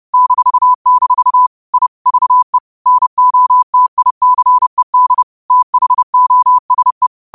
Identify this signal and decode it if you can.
Morse code, heard as a keyed tone, about 30 wpm, 1000 Hz. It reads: -- I VE NOTICED THOSE
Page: Quotes for Wed, 13 Aug 2025 in Morse Code at 30 words per minute.